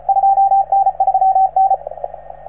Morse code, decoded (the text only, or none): 2N3N